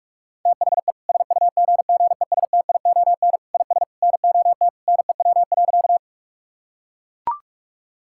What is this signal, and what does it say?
Morse code, heard as a keyed tone, about 41 wpm, 700 Hz. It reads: THE SUGGESTION IS NOT NEW.
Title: Sentences from Top 1600 Words 45wpm